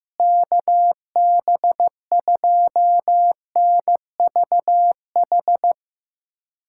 Morse code, heard as a keyed tone, about 15 wpm, 700 Hz. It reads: KB2NVH
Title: USA Callsign DE Callsign - 6 Characters - 15wpm